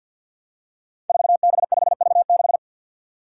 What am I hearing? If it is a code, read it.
Morse code, heard as a keyed tone, about 53 wpm, 700 Hz. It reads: 4B5V6
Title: Five Challenging Letter-Number Combinations 50wpm